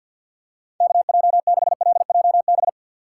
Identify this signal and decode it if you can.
Morse code, heard as a keyed tone, about 50 wpm, 700 Hz. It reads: KJ6LJB
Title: USA Call Signs - 6 Characters - 6x Character Spacing - 50wpm